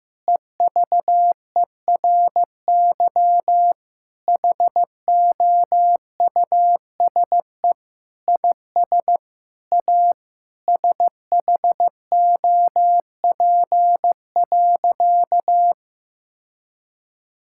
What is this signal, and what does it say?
Morse code, heard as a keyed tone, about 15 wpm, 700 Hz. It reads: EVERY HOUSE IS A SHOP.
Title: Sentences from Top 1600 Words 15wpm